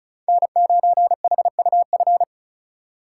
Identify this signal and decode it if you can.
Morse code, heard as a keyed tone, about 34 wpm, 700 Hz. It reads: N9HUF
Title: USA Call Signs - 5 Characters - 4x Character Spacing - 35wpm